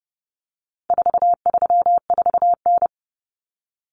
Morse code, heard as a keyed tone, about 30 wpm, 700 Hz. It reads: 434N
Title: Common Cut Numbers - 4-Digit 30wpm